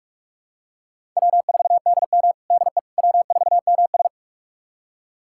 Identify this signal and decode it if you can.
Morse code, heard as a keyed tone, about 44 wpm, 700 Hz. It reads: W4DM DE W4MS
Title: USA Callsign DE Callsign - 4 Characters - 45wpm